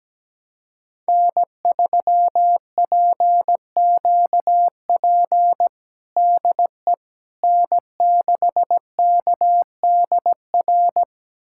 Morse code, heard as a keyed tone, about 17 wpm, 700 Hz. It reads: N3PQP DE N6KDR
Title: USA Callsign DE Callsign - 5 Characters - 17wpm